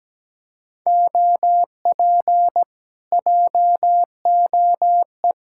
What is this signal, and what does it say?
Morse code, heard as a keyed tone, about 17 wpm, 700 Hz. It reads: OP JOE
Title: Speed-Racing Common QSO Phrases - 17wpm